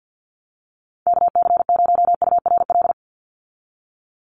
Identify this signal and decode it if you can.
Morse code, heard as a keyed tone, about 50 wpm, 700 Hz. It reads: KC0URL
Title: Call Signs - 50wpm